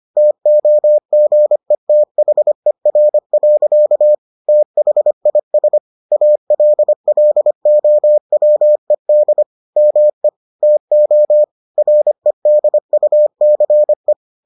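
Morse code, heard as a keyed tone, about 25 wpm, 600 Hz. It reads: TOGETHER. THIS ALLOWED ME TO REDUCE
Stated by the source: Morse edition of the Foundations of Amateur Radio episodes.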